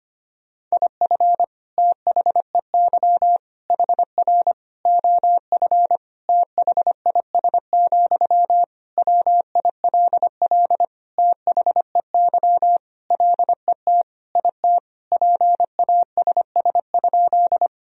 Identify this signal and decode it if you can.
Morse code, heard as a keyed tone, about 25 wpm, 700 Hz. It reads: IF THEY HR OF THIS, WILL THEY LET IT PASS?
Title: Sentences from Top 400 Words and Abbreviations 25wpm